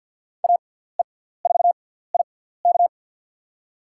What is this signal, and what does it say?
Morse code, heard as a keyed tone, about 52 wpm, spaced as for 17 wpm, 700 Hz. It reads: AE4IK
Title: USA Call Signs - 6x Character Spacing - 50wpm